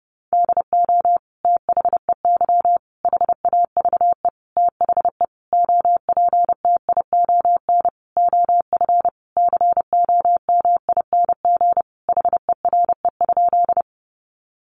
Morse code, heard as a keyed tone, about 30 wpm, 700 Hz. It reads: DO THEY HAVE THE OPTION OF COMING HERE?
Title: Sentences from Top 1300 Words 30wpm